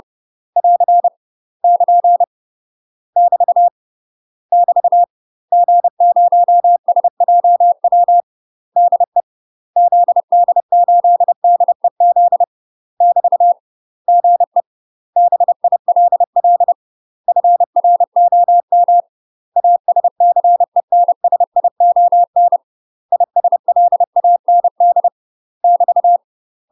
Morse code, heard as a keyed tone, about 30 wpm, 700 Hz. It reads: <AR> <KN> <BT> <BT> G0SJW DE ZD8DEZ <BT> GE BILL FROM ASCENSION ISLAND <BT>
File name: UK MASTER QSOs TRAINING / 30WPM / UK MASTER QSOs 0022